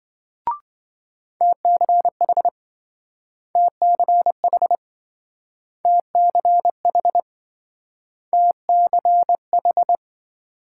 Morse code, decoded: TCH TCH TCH TCH

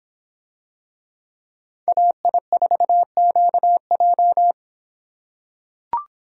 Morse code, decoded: AI4QJ

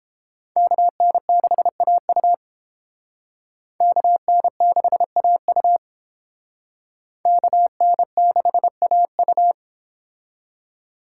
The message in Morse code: KN6AU KN6AU KN6AU

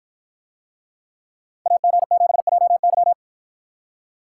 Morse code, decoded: AG7JY

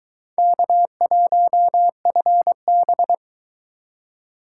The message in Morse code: K1FB

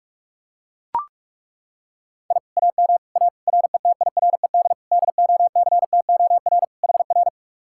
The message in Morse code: I AM A RETIRED DOCTOR HR

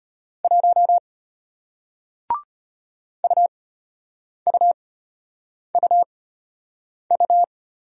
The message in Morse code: 1 UUUU

